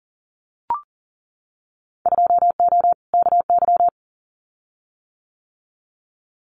Morse code, heard as a keyed tone, about 40 wpm, 700 Hz. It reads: 2O KY
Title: Winter Field Day Exchanges 40wpm